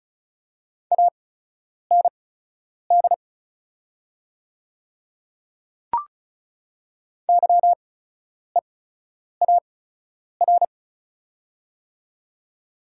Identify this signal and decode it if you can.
Morse code, heard as a keyed tone, about 34 wpm, spaced as for 10 wpm, 700 Hz. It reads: AND YEAR
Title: 8x Character Spacing - Top 300 Words - 35wpm